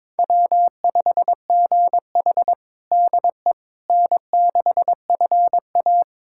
Morse code, heard as a keyed tone, about 22 wpm, 700 Hz. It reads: W5GH DE N6FA
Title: USA Callsign DE Callsign - 4 Characters - 22wpm